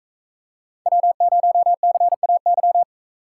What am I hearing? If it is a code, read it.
Morse code, heard as a keyed tone, about 44 wpm, 700 Hz. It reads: W0CAY